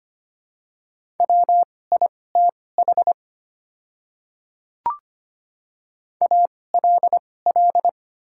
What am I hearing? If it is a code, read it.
Morse code, heard as a keyed tone, about 25 wpm, spaced as for 18 wpm, 700 Hz. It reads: WITH ALL
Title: 2x Character Spacing - Top 100 Words - 25wpm